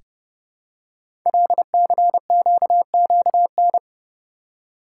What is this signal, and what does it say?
Morse code, heard as a keyed tone, about 29 wpm, 700 Hz. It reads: LCQQN
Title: Code Groups - 5 Letters - 30wpm